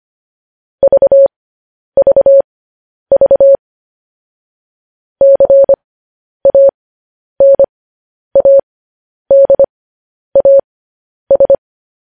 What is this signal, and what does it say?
Morse code, heard as a keyed tone, about 25 wpm, spaced as for 10 wpm, 550 Hz. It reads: VVV CANADAS
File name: podcast / News headlines-1x-25x10